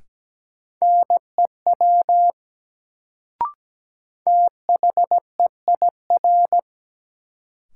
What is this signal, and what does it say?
Morse code, heard as a keyed tone, about 17 wpm, 700 Hz. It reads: NEW THEIR
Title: Sentences from Top 300 Words - New Word Drill - 17wpm